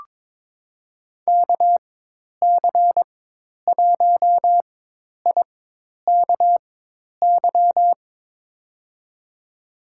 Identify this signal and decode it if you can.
Morse code, heard as a keyed tone, about 22 wpm, spaced as for 10 wpm, 700 Hz. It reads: KC1IKY
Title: USA Call Signs - 6 Characters - 4x Character Spacing - 22wpm